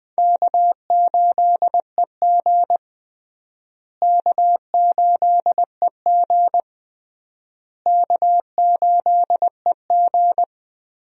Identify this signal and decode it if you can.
Morse code, heard as a keyed tone, about 20 wpm, 700 Hz. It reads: K8EG K8EG K8EG
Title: USA Call Signs - 4 Characters - Repeated 3x - 20wpm